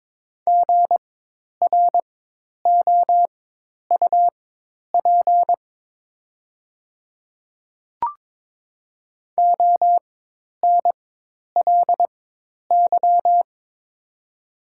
Morse code, heard as a keyed tone, about 22 wpm, spaced as for 10 wpm, 700 Hz. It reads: GROUP ONLY
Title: 4x Character Spacing - Top 500 Words - 22wpm